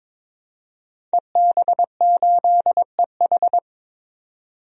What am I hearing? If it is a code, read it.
Morse code, heard as a keyed tone, about 21 wpm, 700 Hz. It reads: EB8EH
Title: DXCC-100 Call Signs - 5 Characters - 22wpm